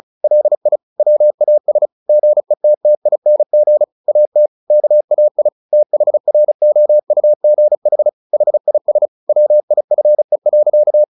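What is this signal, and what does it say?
Morse code, heard as a keyed tone, about 34 wpm, 600 Hz. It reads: LI WAS GETTING AT KAI THROUGH HIS WIFE.
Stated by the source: Morse practice at 35 wpm from CW med Gnister, sending Ken Follett's novel Never in English.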